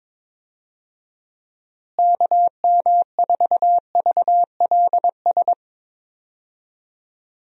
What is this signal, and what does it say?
Morse code, heard as a keyed tone, about 22 wpm, 700 Hz. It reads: KM4VLS
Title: USA Call Signs - 6 Characters - Repeated 3x - 22wpm